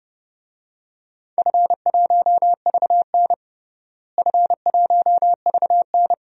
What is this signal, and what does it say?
Morse code, heard as a keyed tone, about 30 wpm, 700 Hz. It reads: F1VN F1VN
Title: DXCC-100 Call Signs - 4 Characters - Repeated 3x - 30wpm